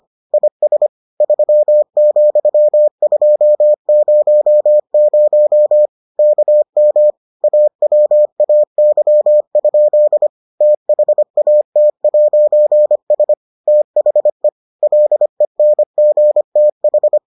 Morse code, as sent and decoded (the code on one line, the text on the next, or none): .. ... / ...-- --..-- ..--- ----- ----- / -.- -- / .- .-- .- -.-- ..--.. / - .... .- - .----. ... / - .... / .-.. -. --. - ....
IS 3,200 KM AWAY? THAT'S THE LENGTH